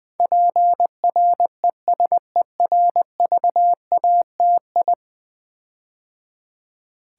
Morse code, heard as a keyed tone, about 20 wpm, 700 Hz. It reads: PRESERVATI
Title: Build A Word - 5 Letters At A Time - 12-Letter Words - 20wpm